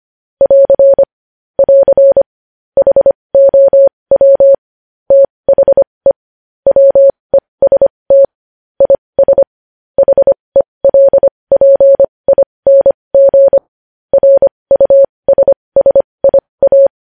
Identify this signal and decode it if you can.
Morse code, heard as a keyed tone, about 25 wpm, spaced as for 20 wpm, 550 Hz. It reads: <AR> <AR> HOW THE WEST IS HELPING RUSSIA
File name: podcast / News headlines-1x-25x20